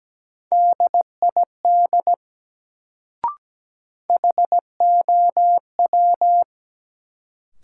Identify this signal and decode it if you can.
Morse code, heard as a keyed tone, about 17 wpm, 700 Hz. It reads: DID HOW
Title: Sentences from Top 100 Words - New Word Drill - 17wpm